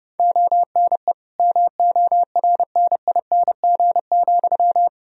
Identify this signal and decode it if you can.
Morse code, heard as a keyed tone, about 30 wpm, 700 Hz. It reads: ONE MORNING,
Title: Sentences from Top 1000 Words - Review Part 2 - 30wpm